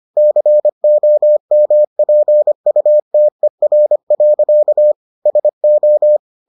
Morse code, heard as a keyed tone, about 25 wpm, 600 Hz. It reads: COMPUTER. SO